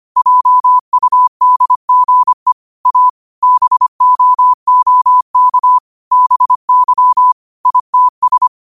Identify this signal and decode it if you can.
Morse code, heard as a keyed tone, about 25 wpm, 1000 Hz. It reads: JUDGE A BOOK BY ITS
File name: qotdfiles / 25wpm